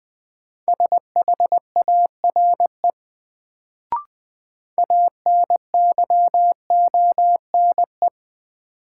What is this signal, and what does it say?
Morse code, heard as a keyed tone, about 20 wpm, 700 Hz. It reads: SHARE ANYONE
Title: Sentences from Top 900 Words - New Word Drill - 20wpm